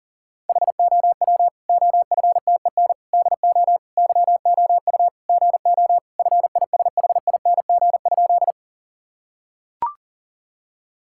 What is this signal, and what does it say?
Morse code, heard as a keyed tone, about 37 wpm, 700 Hz. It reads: HOW OFTEN DO YOU GO FISHING?
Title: Sentences from Top 1000 Words - Review Part 5 - 40wpm